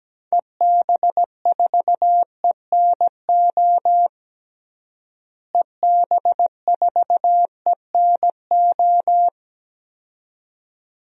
Morse code, heard as a keyed tone, about 17 wpm, 700 Hz. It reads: EB4ENO EB4ENO
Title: DXCC-100 Call Signs - 6 Characters - Repeated 2x - 17wpm